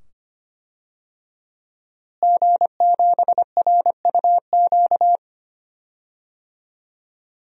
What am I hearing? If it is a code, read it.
Morse code, heard as a keyed tone, about 25 wpm, 700 Hz. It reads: G7RUQ
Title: DXCC-100 Call Signs - 5 Characters - Repeated 2x - 25wpm